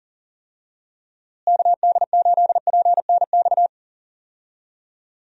Morse code, heard as a keyed tone, about 40 wpm, 700 Hz. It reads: KD8PNX
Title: USA Call Signs - 6 Characters - Repeated 3x - 40wpm